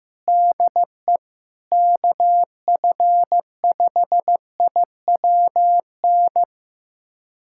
Morse code, heard as a keyed tone, about 15 wpm, 700 Hz. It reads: DE KF5IWN
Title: USA Callsign DE Callsign - 6 Characters - 15wpm